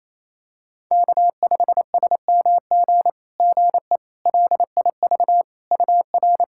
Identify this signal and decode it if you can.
Morse code, heard as a keyed tone, about 28 wpm, 700 Hz. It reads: K5SMG GE LIV UR
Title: SOTA QSO Practice 28wpm